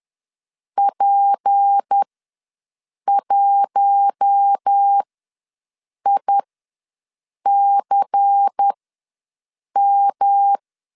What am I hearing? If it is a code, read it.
Morse code, heard as a keyed tone, about 10 wpm, spaced as for 6 wpm, 800 Hz. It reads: P1ICM
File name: LEZIONI / 6 PKX0 / 6 P ws 10wpm 3751